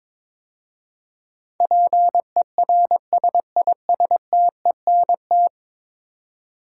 Morse code, heard as a keyed tone, about 22 wpm, 700 Hz. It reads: PERSISTENT
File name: practice / 22wpm / 10 Letter Words - First 3 Letters Given - 22wpm